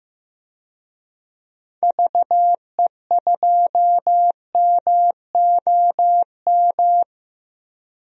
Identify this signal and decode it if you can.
Morse code, heard as a keyed tone, about 15 wpm, 700 Hz. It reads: VE2MOM
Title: DXCC-100 Call Signs - 6 Characters - 15wpm